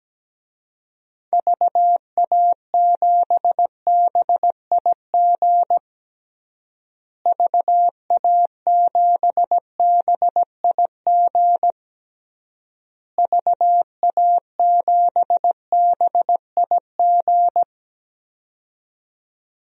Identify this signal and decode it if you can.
Morse code, heard as a keyed tone, about 17 wpm, 700 Hz. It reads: VA7BIG VA7BIG VA7BIG
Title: DXCC-100 Call Signs - 6 Characters - Repeated 3x - 17wpm